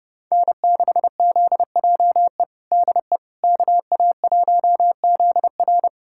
Morse code, heard as a keyed tone, about 30 wpm, 700 Hz. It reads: N6ZJE DE KA1ZR
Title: USA Callsign DE Callsign - 5 Characters - 30wpm